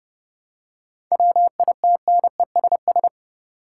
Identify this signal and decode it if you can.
Morse code, heard as a keyed tone, about 30 wpm, 700 Hz. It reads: WITNESS
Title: Seven Letter Words - 30wpm